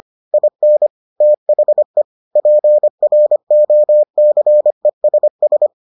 Morse code, heard as a keyed tone, about 25 wpm, 600 Hz. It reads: IN THE PROCESS